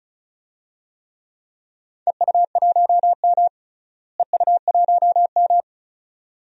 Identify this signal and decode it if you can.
Morse code, heard as a keyed tone, about 36 wpm, 700 Hz. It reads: EU1M EU1M
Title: DXCC-100 Call Signs - 4 Characters - Repeated 2x - 35wpm